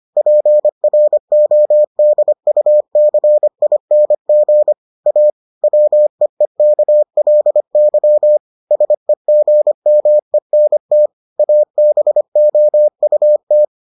Morse code, heard as a keyed tone, about 25 wpm, 600 Hz. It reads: PRODUCING A WEEKLY SEGMENT ABOUT